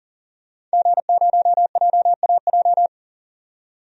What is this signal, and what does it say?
Morse code, heard as a keyed tone, about 40 wpm, 700 Hz. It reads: G0JAJ
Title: DXCC-100 Call Signs - 5 Characters - 40wpm